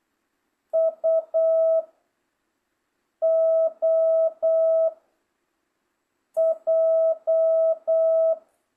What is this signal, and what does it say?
Morse code, heard as a keyed tone, about 8 wpm, 660 Hz. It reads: U O J